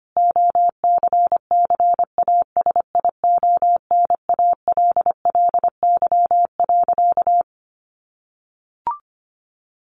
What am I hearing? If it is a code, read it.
Morse code, heard as a keyed tone, about 25 wpm, 700 Hz. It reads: OCCASIONALLY.
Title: Sentences from Top 1300 Words 25wpm